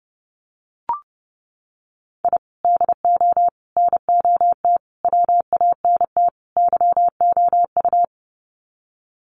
Morse code, heard as a keyed tone, about 30 wpm, 700 Hz. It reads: I DO NOT WANT YOU
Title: Sets of 5 Words 30wpm